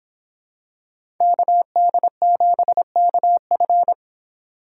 Morse code, heard as a keyed tone, about 26 wpm, 700 Hz. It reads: KD7KF